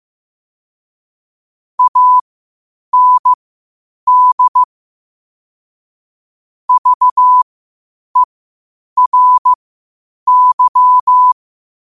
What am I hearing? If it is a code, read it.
Morse code, heard as a keyed tone, about 15 wpm, spaced as for 8 wpm, 1000 Hz. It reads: AND VERY
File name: qotdfiles / 8wpm